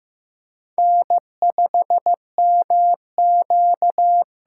Morse code, heard as a keyed tone, about 15 wpm, 700 Hz. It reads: N5MQ
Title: USA Call Signs - 5 Characters - 6x Character Spacing - 15wpm